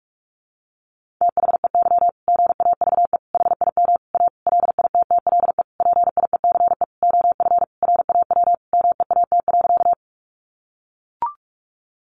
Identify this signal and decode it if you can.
Morse code, heard as a keyed tone, about 44 wpm, 700 Hz. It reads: THEY GAVE HIM A LITTLE PIECE OF RAW MEAT.
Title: Sentences from Top 1500 Words 45wpm